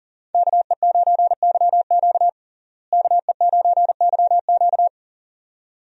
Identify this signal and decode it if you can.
Morse code, heard as a keyed tone, about 40 wpm, 700 Hz. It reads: KE9YQ KE9YQ
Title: USA Call Signs - 5 Characters - Repeated 2x - 40wpm